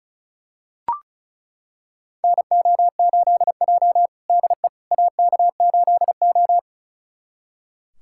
Morse code, heard as a keyed tone, about 36 wpm, 700 Hz. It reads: NO8J DE AK8O
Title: USA Callsign DE Callsign - 4 Characters - 35wpm